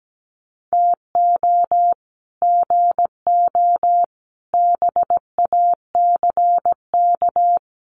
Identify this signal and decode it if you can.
Morse code, heard as a keyed tone, about 17 wpm, 700 Hz. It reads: TO GO BACK
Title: Sets of 4 Words 17wpm